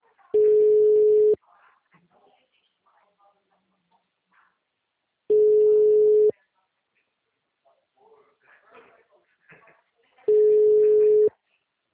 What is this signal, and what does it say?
Ringback tone (Europe, 425 Hz), with speech.
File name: out- -2020- - - .14